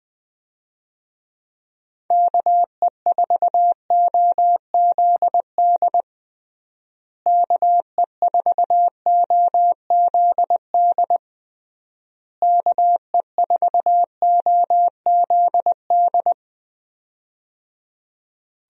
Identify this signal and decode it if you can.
Morse code, heard as a keyed tone, about 21 wpm, 700 Hz. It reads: KE4OZD KE4OZD KE4OZD
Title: USA Call Signs - 6 Characters - Repeated 3x - 20wpm